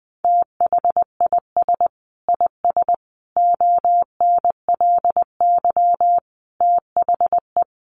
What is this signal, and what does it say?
Morse code, heard as a keyed tone, about 20 wpm, 700 Hz. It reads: THIS IS ONLY THE
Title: Sentences from Top 1300 Words 20wpm